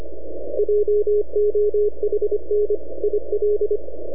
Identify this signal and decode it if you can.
Morse code, heard as a keyed tone, about 25 wpm, 420 Hz, with radio noise. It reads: JOHN IL
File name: NAQPCW-8-2009 / W1UJ NAQP CW 09 1930Z